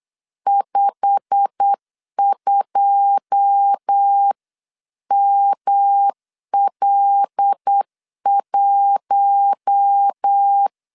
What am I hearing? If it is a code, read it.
Morse code, heard as a keyed tone, about 8 wpm, 800 Hz. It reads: 52ML1
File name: LEZIONI / 9 2 8wpm 3437